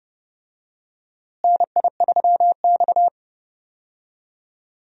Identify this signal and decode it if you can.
Morse code, heard as a keyed tone, about 30 wpm, 700 Hz. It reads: NI3X
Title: USA Call Signs - 4 Characters - Repeated 2x - 30wpm